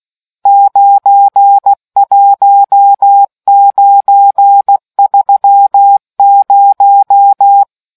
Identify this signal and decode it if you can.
Morse code, heard as a keyed tone, about 16 wpm, 800 Hz. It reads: 91930